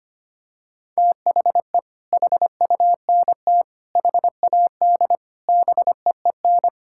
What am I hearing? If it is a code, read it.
Morse code, heard as a keyed tone, about 25 wpm, 700 Hz. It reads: THE HUNT HAD BEEN
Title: Sentences from Top 1600 Words 25wpm